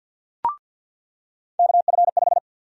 Morse code, heard as a keyed tone, about 47 wpm, 700 Hz. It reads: KF5
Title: USA Call Sign Prefixes - 50wpm